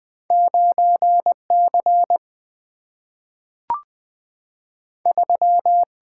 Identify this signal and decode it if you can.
Morse code, heard as a keyed tone, about 20 wpm, 700 Hz. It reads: 9C 3
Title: Single-Letter Single-Number 20wpm